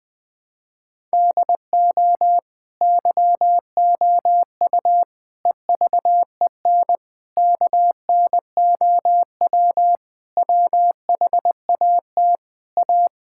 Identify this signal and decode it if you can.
Morse code, heard as a keyed tone, about 20 wpm, 700 Hz. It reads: DO YOU EVEN KNOW WHAT A
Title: Sentences from Top 1000 Words - Review Part 5 - 20wpm